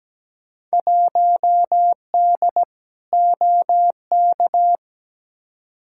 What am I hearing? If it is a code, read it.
Morse code, heard as a keyed tone, about 17 wpm, 700 Hz. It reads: 1D OK
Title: Field Day Exchanges - 17wpm